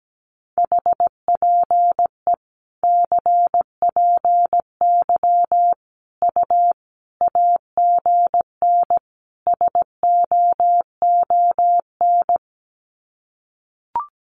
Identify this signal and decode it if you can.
Morse code, heard as a keyed tone, about 17 wpm, 700 Hz. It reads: HPE CPY U AGN SOON
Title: Common QSO Phrases 17wpm